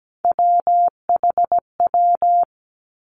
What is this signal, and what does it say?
Morse code, heard as a keyed tone, about 17 wpm, 700 Hz. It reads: WHW
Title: Three Challenging Letter-Number Combinations 17wpm